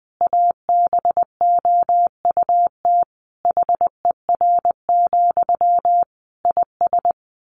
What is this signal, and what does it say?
Morse code, heard as a keyed tone, about 19 wpm, 700 Hz. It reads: ABOUT HER, IS
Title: Sentences from Top 1500 Words 20wpm